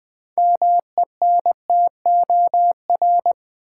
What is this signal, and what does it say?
Morse code, heard as a keyed tone, about 20 wpm, 700 Hz. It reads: MENTOR
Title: Build A Word - 2 Letters At A Time - 6-Letter Words - 20wpm